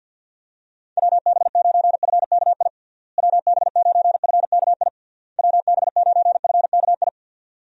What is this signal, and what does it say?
Morse code, heard as a keyed tone, about 51 wpm, 700 Hz. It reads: WB9FKI WB9FKI WB9FKI
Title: USA Call Signs - 6 Characters - Repeated 3x - 50wpm